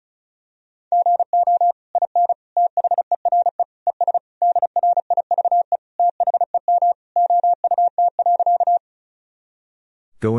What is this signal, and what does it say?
Morse code, heard as a keyed tone, about 36 wpm, 700 Hz. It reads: GO IN THERE ES DRIVE THEM OUT.
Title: Sentences from Top 500 Words and Abbreviations 35wpm